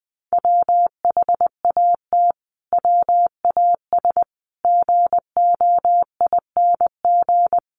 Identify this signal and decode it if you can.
Morse code, heard as a keyed tone, about 19 wpm, 700 Hz. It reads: WHAT WAS GOING
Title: Sets of 5 Words Encore 20wpm